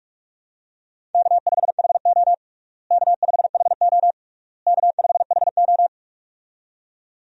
Morse code, heard as a keyed tone, about 41 wpm, 700 Hz. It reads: K5HO K5HO K5HO
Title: USA Call Signs - 4 Characters - Repeated 3x - 45wpm